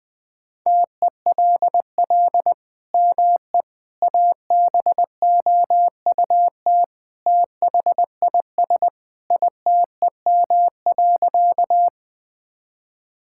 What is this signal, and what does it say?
Morse code, heard as a keyed tone, about 21 wpm, 700 Hz. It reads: TELL ME ABOUT THIS ITEM.
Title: Sentences from Top 1000 Words - Review Part 6 - 20wpm